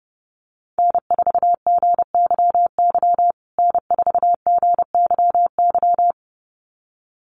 Morse code, heard as a keyed tone, about 30 wpm, 700 Hz. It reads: N4GYY N4GYY
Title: USA Call Signs Repeated 2x 30wpm